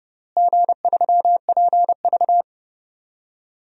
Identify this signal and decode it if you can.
Morse code, heard as a keyed tone, about 30 wpm, 700 Hz. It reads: G3PV
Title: DXCC-100 Call Signs - 4 Characters - Repeated 3x - 30wpm